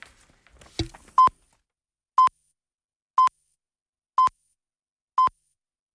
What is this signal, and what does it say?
Radio time signal (time pips). It is heard from 1.2 to 5.7 s.